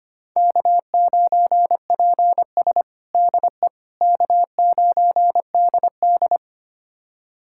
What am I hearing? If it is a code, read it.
Morse code, heard as a keyed tone, about 25 wpm, 700 Hz. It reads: K9PS DE K9DD
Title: USA Callsign DE Callsign - 4 Characters - 25wpm